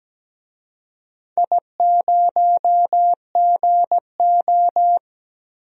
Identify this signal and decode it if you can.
Morse code, heard as a keyed tone, about 17 wpm, 700 Hz. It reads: I0GO
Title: DXCC-100 Call Signs - 4 Characters - 17wpm